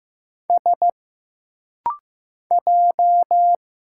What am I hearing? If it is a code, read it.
Morse code, heard as a keyed tone, about 15 wpm, 700 Hz. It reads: S J
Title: Single Letters - Rapid-Fire 15wpm